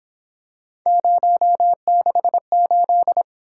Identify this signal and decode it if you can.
Morse code, heard as a keyed tone, about 27 wpm, 700 Hz. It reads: 068
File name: Speed-Racing - Three Letter-Number Combinations - 22wpm